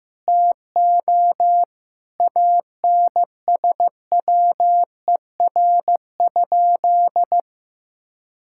Morse code, heard as a keyed tone, about 15 wpm, 700 Hz. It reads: TO ANSWER?
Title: Sentences from Top 1000 Words - Review Part 6 - 15wpm